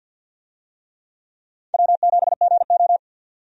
Morse code, heard as a keyed tone, about 49 wpm, 700 Hz. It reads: W7GO
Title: USA Call Signs - 4 Characters - 50wpm